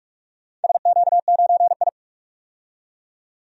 Morse code, heard as a keyed tone, about 46 wpm, 700 Hz. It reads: SQ9I
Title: DXCC-100 Call Signs - 4 Characters - 45wpm